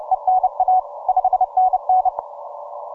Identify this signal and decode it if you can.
Morse code, heard as a keyed tone, about 29 wpm, 740 Hz. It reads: ENA 5NN
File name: simple-file-list / 3D2EU 17MCW